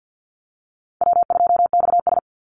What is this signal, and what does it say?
Morse code, heard as a keyed tone, about 48 wpm, 700 Hz. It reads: W2XS
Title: Easy Call Signs - 50wpm